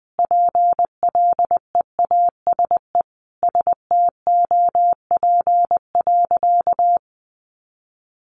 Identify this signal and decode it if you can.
Morse code, heard as a keyed tone, about 20 wpm, 700 Hz. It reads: PLEASE STOP.
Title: Sentences from Top 1500 Words 20wpm